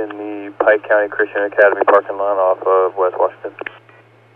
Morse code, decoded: none

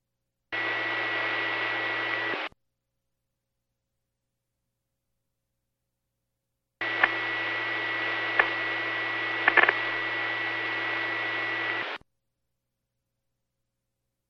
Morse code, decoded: none

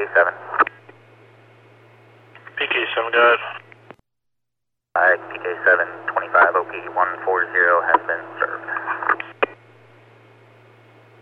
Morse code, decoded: none

